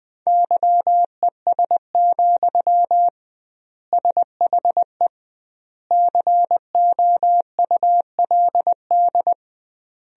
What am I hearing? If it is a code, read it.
Morse code, heard as a keyed tone, about 21 wpm, 700 Hz. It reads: YES, SHE COULD
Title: 2x Word-Spacing - Sentences from Top 500 Words - 20wpm